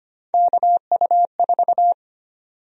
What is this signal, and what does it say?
Morse code, heard as a keyed tone, about 25 wpm, 700 Hz. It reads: KU4
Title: USA Call Sign Prefixes - Repeated 2x - 25wpm